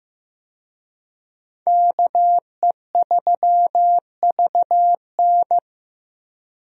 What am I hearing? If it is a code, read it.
Morse code, heard as a keyed tone, about 15 wpm, 700 Hz. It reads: KE3VN
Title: USA Call Signs - 5 Characters - Repeated 3x - 15wpm